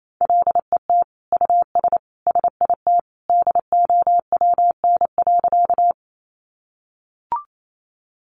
Morse code, heard as a keyed tone, about 28 wpm, 700 Hz. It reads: LET US SIT DOWN.
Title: Sentences from Top 500 Words 28wpm